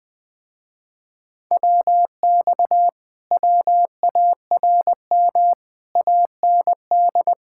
Morse code, heard as a keyed tone, about 21 wpm, 700 Hz. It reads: WX WARM AND
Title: Speed-Racing Common QSO Phrases - 20wpm